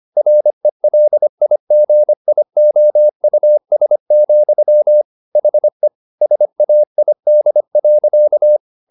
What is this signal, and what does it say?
Morse code, heard as a keyed tone, about 25 wpm, 600 Hz. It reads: RELIGIOUS, HE SAID.
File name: the old man and the sea part 1 25wpm